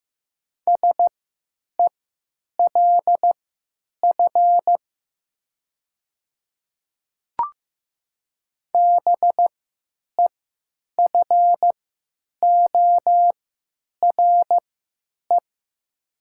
Morse code, heard as a keyed tone, about 15 wpm, spaced as for 9 wpm, 700 Hz. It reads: SELF BEFORE
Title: 3x Character Spacing - Top 300 Words - 15wpm